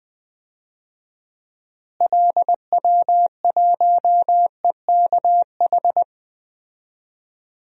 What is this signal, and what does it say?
Morse code, heard as a keyed tone, about 20 wpm, 700 Hz. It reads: LW1EKH
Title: DXCC-100 Call Signs - 6 Characters - Repeated 2x - 20wpm